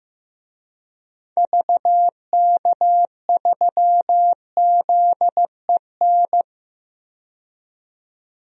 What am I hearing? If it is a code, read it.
Morse code, heard as a keyed tone, about 15 wpm, 690 Hz. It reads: VK3ZEN
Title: DXCC-100 Call Signs - 6 Characters - 15wpm